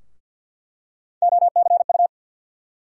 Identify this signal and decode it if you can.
Morse code, heard as a keyed tone, about 46 wpm, 700 Hz. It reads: OCU
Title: USA Call Sign Postfixes - 3 Letters 50wpm